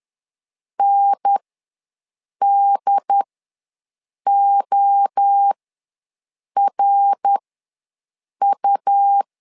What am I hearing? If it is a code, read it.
Morse code, heard as a keyed tone, about 10 wpm, spaced as for 6 wpm, 800 Hz. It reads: NDORU
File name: LEZIONI / 3 MWRO / 3 MWRO ws 10wpm 2310